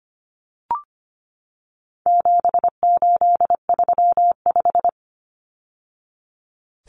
Morse code, heard as a keyed tone, about 25 wpm, 700 Hz. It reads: 7835
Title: Numbers - 4-Digit 25wpm